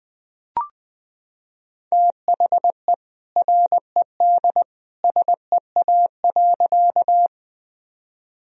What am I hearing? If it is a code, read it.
Morse code, heard as a keyed tone, about 20 wpm, 700 Hz. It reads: THE RED SEA.
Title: Sentences from Top 1000 Words - Review Part 2 - 20wpm